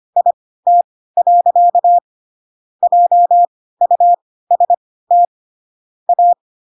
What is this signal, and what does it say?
Morse code, heard as a keyed tone, about 25 wpm, spaced as for 16 wpm, 700 Hz. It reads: IT. JUST A
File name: Pinocchio / Pinocchio ch1 10 25 ik1hge